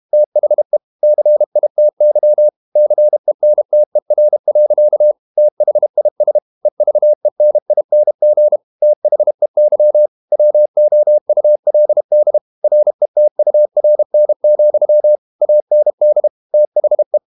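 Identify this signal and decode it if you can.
Morse code, heard as a keyed tone, about 32 wpm, 600 Hz. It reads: THE CITY CENTER. THIS EVENING THEY WOULD RETURN, AND THE
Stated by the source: Morse code practice at 32 wpm from CW med Gnister, reading Ken Follett's novel Never in English.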